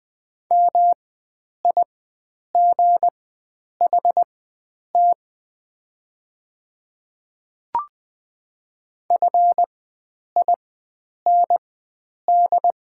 Morse code, heard as a keyed tone, about 20 wpm, spaced as for 9 wpm, 700 Hz. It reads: MIGHT FIND
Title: 4x Character Spacing - Top 300 Words - 20wpm